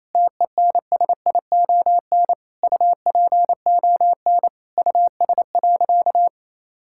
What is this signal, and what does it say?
Morse code, heard as a keyed tone, about 28 wpm, 700 Hz. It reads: TENSION UPON US.
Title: Sentences from Top 1600 Words 28wpm